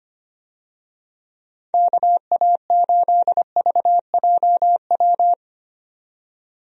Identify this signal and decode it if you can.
Morse code, heard as a keyed tone, about 25 wpm, 700 Hz. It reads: KA8VJW
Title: USA Call Signs - 6 Characters - 25wpm